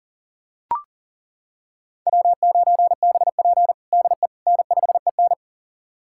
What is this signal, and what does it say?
Morse code, heard as a keyed tone, about 40 wpm, 700 Hz. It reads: W9BP DE N5EN
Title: USA Callsign DE Callsign - 4 Characters - 40wpm